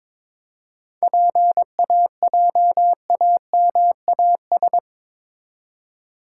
Morse code, heard as a keyed tone, about 28 wpm, 700 Hz. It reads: PAJAMAS